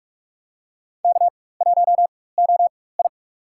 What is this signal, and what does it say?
Morse code, heard as a keyed tone, about 46 wpm, spaced as for 21 wpm, 700 Hz. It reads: K1OI
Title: USA Call Signs - 4 Characters - 4x Character Spacing - 45wpm